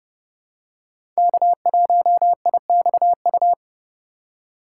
Morse code, heard as a keyed tone, about 30 wpm, 700 Hz. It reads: K1IXU
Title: USA Call Signs - 3x Character Spacing - 30wpm